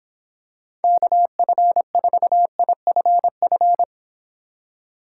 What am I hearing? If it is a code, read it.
Morse code, heard as a keyed tone, about 25 wpm, 700 Hz. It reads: KF4IFF